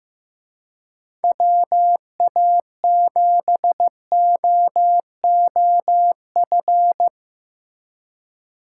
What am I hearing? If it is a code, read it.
Morse code, heard as a keyed tone, about 17 wpm, 700 Hz. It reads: WA7OOF
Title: Speed Racing - USA Call Signs 15wpm